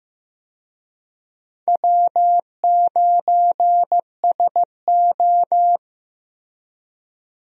Morse code, heard as a keyed tone, about 15 wpm, 700 Hz. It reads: W9SO
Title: USA Call Signs - 4 Characters - Repeated 3x - 15wpm